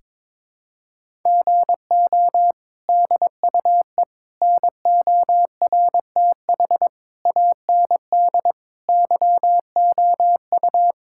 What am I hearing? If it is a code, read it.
Morse code, heard as a keyed tone, about 22 wpm, 700 Hz. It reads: GO DUE NORTH AND YOU
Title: Sentences from Top 1000 Words - Review Part 6 - 22wpm